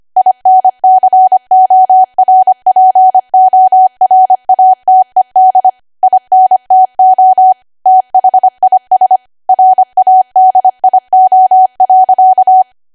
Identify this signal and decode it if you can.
Morse code, heard as a keyed tone, about 25 wpm, 750 Hz. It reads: INCORPORATED INTO THIS RADIO.
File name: Archive / 25 WPM / 25WPM